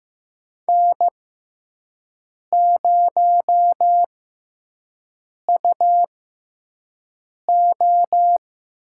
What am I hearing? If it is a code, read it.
Morse code, heard as a keyed tone, about 15 wpm, spaced as for 5 wpm, 700 Hz. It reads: N0UO
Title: USA Call Signs - 5 Characters - 6x Character Spacing - 15wpm